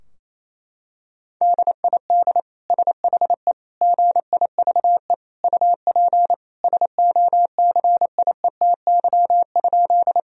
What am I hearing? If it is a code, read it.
Morse code, heard as a keyed tone, about 28 wpm, 700 Hz. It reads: DID SHE GIVE UP SOCIETY?
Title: Sentences from Top 1000 Words - Review Part 4 - 28wpm